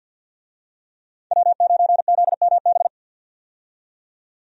Morse code, heard as a keyed tone, about 48 wpm, 700 Hz. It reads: W9ZMB
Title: USA Call Signs - 5 Characters - 50wpm